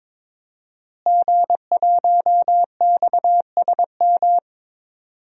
Morse code, heard as a keyed tone, about 22 wpm, 700 Hz. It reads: G1XSM
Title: DXCC-100 Call Signs - 5 Characters - Repeated 2x - 22wpm